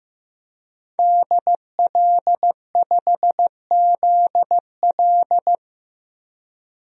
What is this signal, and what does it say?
Morse code, heard as a keyed tone, about 15 wpm, 700 Hz. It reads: DL5ZL
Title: DXCC-100 Call Signs - 5 Characters - Repeated 3x - 15wpm